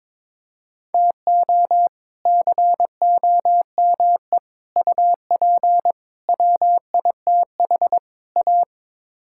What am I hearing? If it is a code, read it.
Morse code, heard as a keyed tone, about 22 wpm, 700 Hz. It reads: TO COME UP WITH A
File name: practice / 22wpm / Build A Phrase - 5-Word Phrases - 22wpm